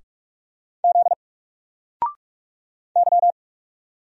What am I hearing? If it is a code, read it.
Morse code, heard as a keyed tone, about 44 wpm, 700 Hz. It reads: Z Y
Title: Single Character (Letter:Number:Punctuation:Prosign) Alt 45wpm